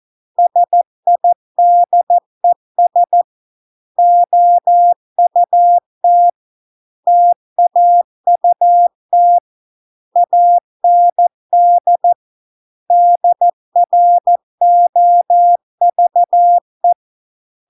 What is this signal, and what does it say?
Morse code, heard as a keyed tone, about 14 wpm, 700 Hz. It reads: SIDES OUT TAUT AND DROVE